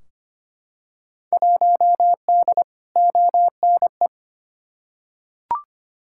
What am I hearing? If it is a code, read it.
Morse code, heard as a keyed tone, about 25 wpm, 700 Hz. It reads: 1D ONE
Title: Field Day Exchanges - 25wpm